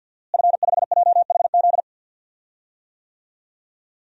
Morse code, heard as a keyed tone, about 48 wpm, 700 Hz. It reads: F5JHZ
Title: DXCC-100 Call Signs - 5 Characters - Repeated 2x - 50wpm